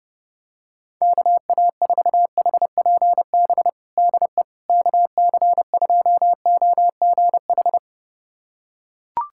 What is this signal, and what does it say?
Morse code, heard as a keyed tone, about 30 wpm, 700 Hz. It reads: KA4HPB DE KC2OGH
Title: USA Callsign DE Callsign 30wpm